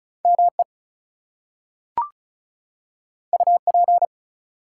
Morse code, heard as a keyed tone, about 36 wpm, 700 Hz. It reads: ME UP
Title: Two Letter Words - 35wpm